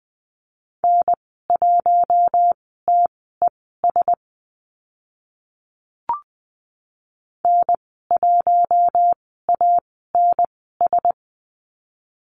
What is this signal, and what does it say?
Morse code, heard as a keyed tone, about 20 wpm, spaced as for 15 wpm, 700 Hz. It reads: N1TES N1ANS
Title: 042 Lesson 8g - Number 1 - Call Signs 20wpm